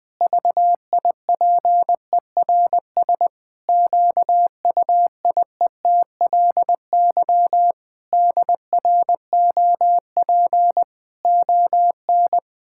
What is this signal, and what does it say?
Morse code, heard as a keyed tone, about 20 wpm, 700 Hz. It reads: VIPERS QUIETLY DROP ON
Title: Pangram Challenge v2 - 20wpm